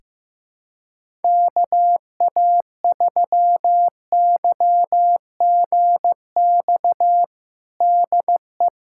Morse code, heard as a keyed tone, about 15 wpm, 700 Hz. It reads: KA3YGX DE
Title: USA Callsign DE Callsign 15wpm